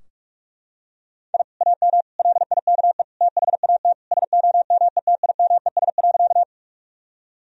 Morse code, heard as a keyed tone, about 41 wpm, 700 Hz. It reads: I AM LIKE THAT SOMETIMES.
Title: Sentences from Top 1000 Words - Review Part 3 - 45wpm